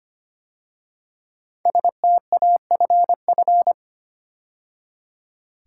Morse code, heard as a keyed tone, about 25 wpm, 700 Hz. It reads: STAFF
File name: 5 Letter Words - First 3 Letters Given - 25wpm